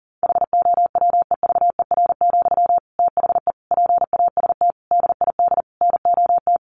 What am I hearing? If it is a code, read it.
Morse code, heard as a keyed tone, about 40 wpm, 700 Hz. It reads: HOWEVER, THE PAST DID NOT
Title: Sentences from Top 1400 Words 40wpm